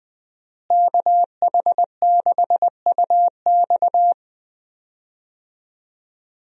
Morse code, heard as a keyed tone, about 20 wpm, 700 Hz. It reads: KH6UX
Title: USA Call Signs - 5 Characters - Repeated 3x - 20wpm